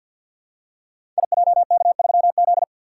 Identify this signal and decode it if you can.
Morse code, heard as a keyed tone, about 46 wpm, 700 Hz. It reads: IJK3Z
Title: Code Groups - 5 Letter:Numbers - 50wpm